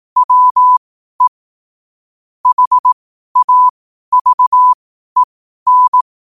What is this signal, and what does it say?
Morse code, heard as a keyed tone, about 18 wpm, spaced as for 12 wpm, 1000 Hz. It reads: WE HAVEN
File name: qotdfiles / 12wpm